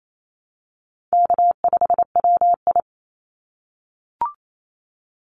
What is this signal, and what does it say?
Morse code, heard as a keyed tone, about 28 wpm, 700 Hz. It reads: K5WI
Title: NA SOTA Chaser Callsigns 28wpm